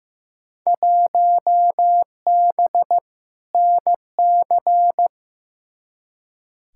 Morse code, heard as a keyed tone, about 15 wpm, 700 Hz. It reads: 1B NC
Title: Field Day Exchanges - 15wpm